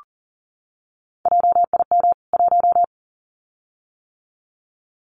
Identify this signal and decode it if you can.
Morse code, heard as a keyed tone, about 39 wpm, 700 Hz. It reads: JIM 1
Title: CWT Exchange 40wpm